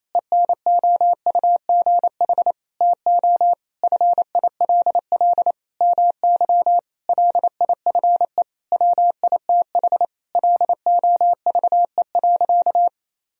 Morse code, heard as a keyed tone, about 28 wpm, 700 Hz. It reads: ENOUGH TO FILL MY LIFE WITH LOVE.
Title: Sentences from Top 1000 Words - Review Part 3 - 28wpm